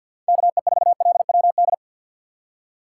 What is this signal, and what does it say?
Morse code, heard as a keyed tone, about 47 wpm, 700 Hz. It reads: KE4LWD
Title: USA Call Signs - 6 Characters - 4x Character Spacing - 50wpm